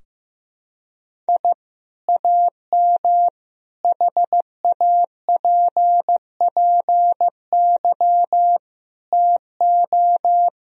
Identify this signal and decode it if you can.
Morse code, heard as a keyed tone, about 15 wpm, 700 Hz. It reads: I AM HAPPY TO
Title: Sentences from Top 1000 Words - Review Part 3 - 15wpm